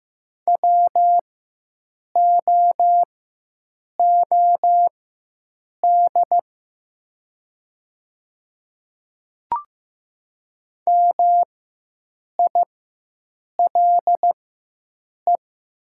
Morse code, heard as a keyed tone, about 15 wpm, spaced as for 7 wpm, 700 Hz. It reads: WOOD MILE
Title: 4x Character Spacing - Top 300 Words - 15wpm